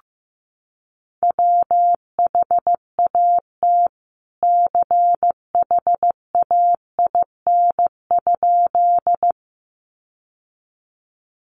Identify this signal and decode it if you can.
Morse code, heard as a keyed tone, about 15 wpm, 700 Hz. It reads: WHAT CHAIN?
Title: Sentences from Top 1500 Words 15wpm